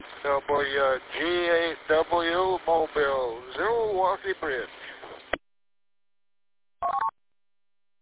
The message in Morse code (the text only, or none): none